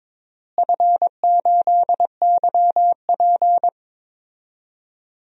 Morse code, heard as a keyed tone, about 22 wpm, 700 Hz. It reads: F8YP
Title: DXCC-100 Call Signs - 4 Characters - Repeated 2x - 22wpm